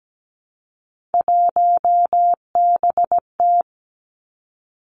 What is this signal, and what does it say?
Morse code, heard as a keyed tone, about 17 wpm, 700 Hz. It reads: 1BT
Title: Three Challenging Letter-Number Combinations 17wpm